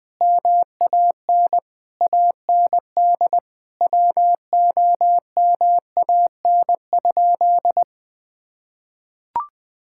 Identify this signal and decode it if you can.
Morse code, heard as a keyed tone, about 20 wpm, 700 Hz. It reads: MAN AND WOMAN?
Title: Sentences from Top 1600 Words 20wpm